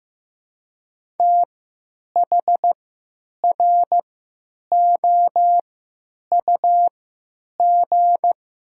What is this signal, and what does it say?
Morse code, heard as a keyed tone, about 15 wpm, 700 Hz. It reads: T H R O U G
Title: 3x Character Spacing - Top 300 Words - 15wpm